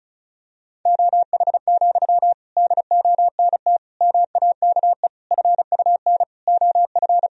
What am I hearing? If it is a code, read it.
Morse code, heard as a keyed tone, about 36 wpm, 700 Hz. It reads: OH, DONT MAKE FUN OF
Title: Sentences from Top 1000 Words - Review Part 4 - 35wpm